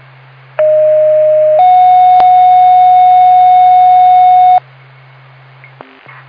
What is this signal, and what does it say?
Two-tone page, heard as a one-second tone, then a three-second tone, about 620 then 730 Hz. It is heard from 0.6 to 4.6 s.
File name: Byram FD